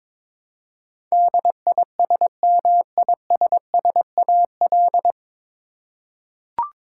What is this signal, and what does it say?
Morse code, heard as a keyed tone, about 22 wpm, 700 Hz. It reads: DISMISSAL
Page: Morse Code Ninja practice 22wpm